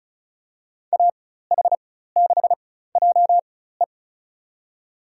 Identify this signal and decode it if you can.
Morse code, heard as a keyed tone, about 34 wpm, spaced as for 16 wpm, 700 Hz. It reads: AH6JE
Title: USA Call Signs - 5 Characters - 4x Character Spacing - 35wpm